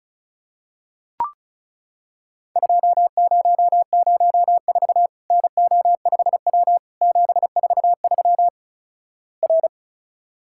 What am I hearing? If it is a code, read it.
Morse code, heard as a keyed tone, about 36 wpm, 700 Hz. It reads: 2004 NO5W 743 R
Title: Worked All Europe DX Contest - QTC Exchange - 35wpm